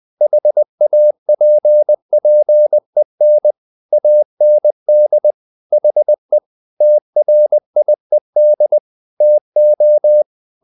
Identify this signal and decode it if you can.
Morse code, heard as a keyed tone, about 20 wpm, 600 Hz. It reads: HAPPEN AND HE TRIED TO
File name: the old man and the sea part 2 20wpm